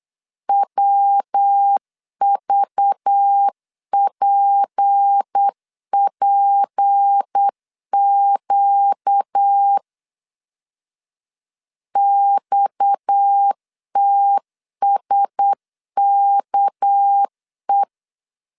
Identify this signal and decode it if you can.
Morse code, heard as a keyed tone, about 8 wpm, 800 Hz. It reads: WVPPQ XTSKE
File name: CW / DET / 1 6 1500C S B 8wpm 5142